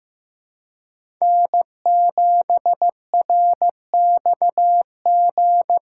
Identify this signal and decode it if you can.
Morse code, heard as a keyed tone, about 15 wpm, 700 Hz. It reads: N7RXG
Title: USA Call Signs - 5 Characters - 2x Character Spacing - 15wpm